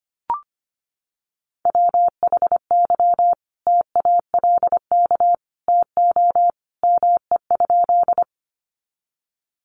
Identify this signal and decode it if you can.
Morse code, heard as a keyed tone, about 25 wpm, 700 Hz. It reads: WHY TALK TO ME?
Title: Sentences from Top 400 Words 25wpm